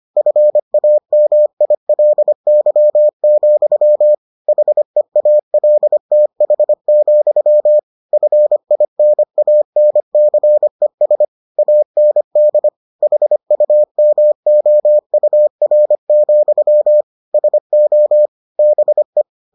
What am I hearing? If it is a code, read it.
Morse code, heard as a keyed tone, about 25 wpm, 600 Hz. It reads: FAMILY, HEALTH, FINANCES AND HUMOUR, SO BE